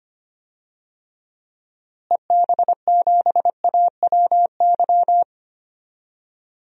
Morse code, heard as a keyed tone, about 25 wpm, 700 Hz. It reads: EB7AWY
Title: DXCC-100 Call Signs - 6 Characters - Repeated 2x - 25wpm